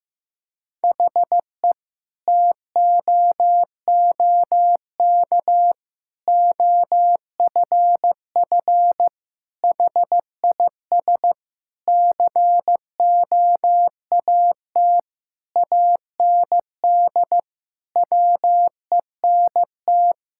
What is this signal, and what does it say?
Morse code, heard as a keyed tone, about 15 wpm, 700 Hz. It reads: HE TOOK OFF HIS COAT AND WENT
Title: Sentences from Top 1600 Words 15wpm